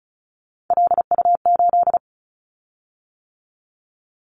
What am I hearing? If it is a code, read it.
Morse code, heard as a keyed tone, about 34 wpm, 700 Hz. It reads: LU8
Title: Callsign Prefixes for Top 100 DXCC - 35wpm